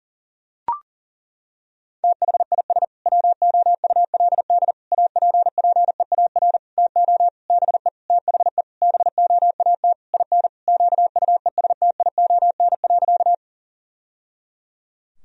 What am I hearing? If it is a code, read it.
Morse code, heard as a keyed tone, about 37 wpm, 700 Hz. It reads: THIS WOULD APPEAR TO BE THE BOAT IN QUESTION.
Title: Sentences from Top 1000 Words - Review Part 3 - 40wpm